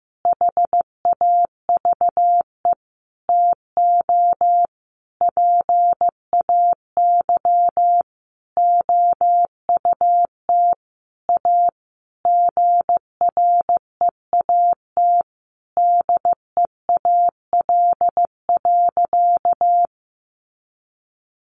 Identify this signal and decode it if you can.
Morse code, heard as a keyed tone, about 15 wpm, 700 Hz. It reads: HAVE TO PAY OUT A GREAT DEAL.
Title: Sentences from Top 1500 Words 15wpm